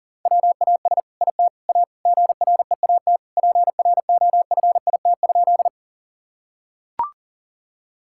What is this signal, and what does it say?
Morse code, heard as a keyed tone, about 40 wpm, 690 Hz. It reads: WAS IT A GREAT PROFIT?
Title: Sentences from Top 1000 Words - Review Part 5 - 40wpm